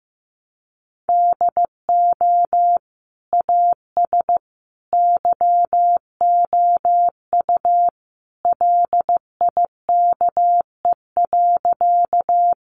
Morse code, heard as a keyed tone, about 15 wpm, 700 Hz. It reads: DO AS YOU LIKE.
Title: Sentences from Top 100 Words 15wpm